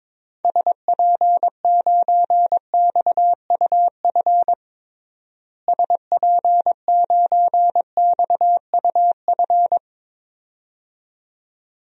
Morse code, heard as a keyed tone, about 22 wpm, 700 Hz. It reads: SP9XUF SP9XUF
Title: DXCC-100 Call Signs - 6 Characters - Repeated 2x - 22wpm